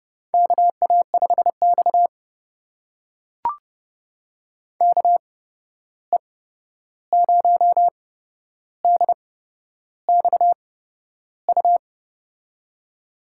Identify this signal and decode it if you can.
Morse code, heard as a keyed tone, about 30 wpm, spaced as for 11 wpm, 700 Hz. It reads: KA5X KE0DXU
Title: USA Call Signs - 8x Character Spacing - 30wpm